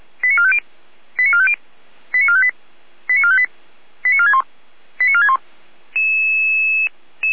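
Five-tone selective call (ZVEI), heard as quick runs of five tones, tones of about 70 ms. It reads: 89450 89450 89458 89458 89461 89461
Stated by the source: ZVEI five-tone call received by SDR.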